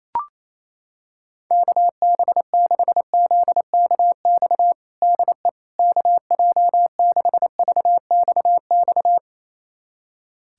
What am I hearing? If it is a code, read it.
Morse code, heard as a keyed tone, about 28 wpm, 700 Hz. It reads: KB6ZKX DE KJ6VXX
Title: USA Callsign DE Callsign - 6 Characters - 28wpm